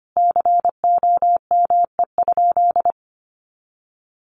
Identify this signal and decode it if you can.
Morse code, heard as a keyed tone, about 25 wpm, 700 Hz. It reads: COME?
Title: Sentences from Top 1400 Words 25wpm